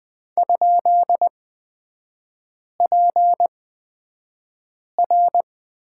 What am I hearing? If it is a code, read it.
Morse code, heard as a keyed tone, about 20 wpm, spaced as for 5 wpm, 700 Hz. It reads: ?PR
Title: 224 Lesson 36e - BT - Three Characters 20wpm